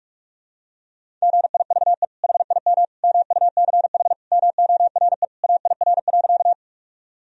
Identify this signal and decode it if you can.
Morse code, heard as a keyed tone, about 47 wpm, 700 Hz. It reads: GIVE HIM MUCH MORE AIR.
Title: Sentences from Top 1000 Words - Review Part 1 - 45wpm